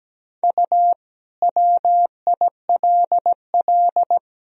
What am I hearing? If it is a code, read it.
Morse code, heard as a keyed tone, about 17 wpm, 700 Hz. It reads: U WILL
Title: Sentences from Top 400 Words and Abbreviations 17wpm